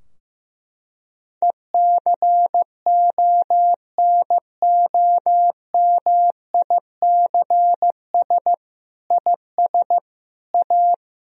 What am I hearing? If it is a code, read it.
Morse code, heard as a keyed tone, about 15 wpm, 700 Hz. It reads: ECONOMICS IS A
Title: Sentences from Top 1000 Words - Review Part 3 - 15wpm